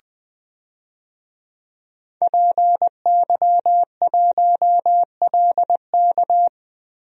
Morse code, heard as a keyed tone, about 20 wpm, 700 Hz. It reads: PY1LK
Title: DXCC-100 Call Signs - 5 Characters - 20wpm